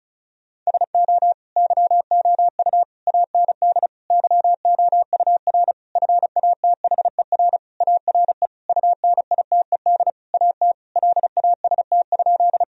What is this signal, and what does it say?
Morse code, heard as a keyed tone, about 36 wpm, 700 Hz. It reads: SO YOU AND YOUR FATHER ARE UNITED AT LAST?
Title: Sentences from Top 1000 Words - Review Part 4 - 35wpm